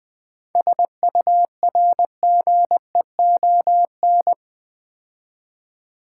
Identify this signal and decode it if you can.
Morse code, heard as a keyed tone, about 20 wpm, 700 Hz. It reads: SURGEON
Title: 7 Letter Words - First 3 Letters Given - 20wpm